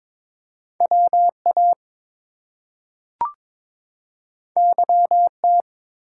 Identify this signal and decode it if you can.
Morse code, heard as a keyed tone, about 22 wpm, 700 Hz. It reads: WA YT
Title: US States and Canadian Provinces 22wpm